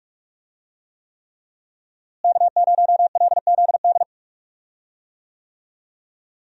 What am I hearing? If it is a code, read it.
Morse code, heard as a keyed tone, about 46 wpm, 700 Hz. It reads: K0LZD
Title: USA Call Signs - 5 Characters - Repeated 3x - 45wpm